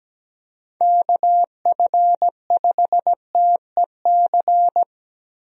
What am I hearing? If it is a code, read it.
Morse code, heard as a keyed tone, about 17 wpm, 700 Hz. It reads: KF5TEC
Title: USA Call Signs - 6 Characters - 17wpm